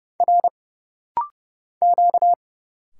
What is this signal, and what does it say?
Morse code, heard as a keyed tone, about 31 wpm, 700 Hz. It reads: R Q
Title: Single Letters - Rapid-Fire 30wpm